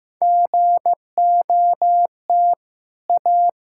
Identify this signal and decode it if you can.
Morse code, heard as a keyed tone, about 15 wpm, 700 Hz. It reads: GOT A
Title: Sentences from Top 1600 Words 15wpm